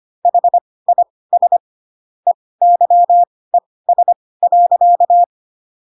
Morse code, heard as a keyed tone, about 25 wpm, spaced as for 18 wpm, 700 Hz. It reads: HIS EYES.
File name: Pinocchio / Pinocchio ch6 12 25 ik1hge